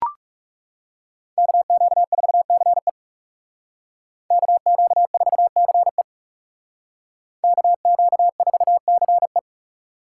Morse code, heard as a keyed tone, about 38 wpm, 700 Hz. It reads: KQ4CE KQ4CE KQ4CE